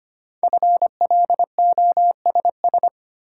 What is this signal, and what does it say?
Morse code, heard as a keyed tone, about 25 wpm, 700 Hz. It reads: FLOSS
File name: practice / 25wpm / Easy Words with Hints - 25wpm